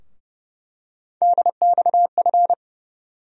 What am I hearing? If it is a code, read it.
Morse code, heard as a keyed tone, about 30 wpm, 700 Hz. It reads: DXF
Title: USA Call Sign Postfixes - 3 Letters 30wpm